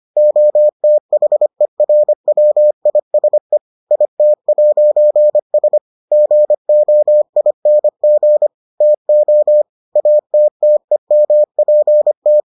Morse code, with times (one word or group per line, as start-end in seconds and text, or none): 0.2-3.6 OTHERWISE
3.9-5.8 IT'S
6.1-8.5 GOING
8.8-9.6 TO
10.0-12.4 ATTEMPT